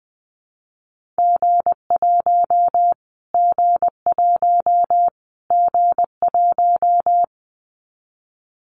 Morse code, heard as a keyed tone, about 20 wpm, 700 Hz. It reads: G1 G1 G1
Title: Callsign Prefixes for Top 25 DXCC - Repeated 3x - 20wpm